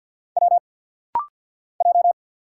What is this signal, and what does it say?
Morse code, heard as a keyed tone, about 50 wpm, 700 Hz. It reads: W J
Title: Single Letters - Rapid-Fire 50wpm